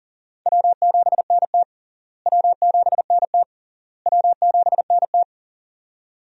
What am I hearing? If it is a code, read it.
Morse code, heard as a keyed tone, about 37 wpm, 700 Hz. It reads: W7NT W7NT W7NT
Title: USA Call Signs - 4 Characters - Repeated 3x - 40wpm